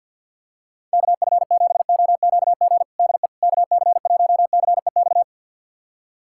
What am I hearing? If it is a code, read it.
Morse code, heard as a keyed tone, about 51 wpm, 700 Hz. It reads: KF7OQG DE KC1CEX
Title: USA Callsign DE Callsign - 6 Characters - 50wpm